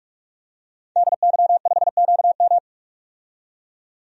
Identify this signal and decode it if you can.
Morse code, heard as a keyed tone, about 46 wpm, 700 Hz. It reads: DY5QM
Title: DXCC-100 Call Signs - 5 Characters - 45wpm